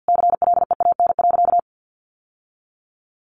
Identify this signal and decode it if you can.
Morse code, heard as a keyed tone, about 50 wpm, 700 Hz. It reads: CLEAN.
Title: Sentences from Top 1100 Words 50wpm